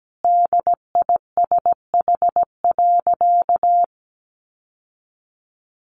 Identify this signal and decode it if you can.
Morse code, heard as a keyed tone, about 17 wpm, 700 Hz. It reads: DISH.
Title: Sentences from Top 1300 Words 17wpm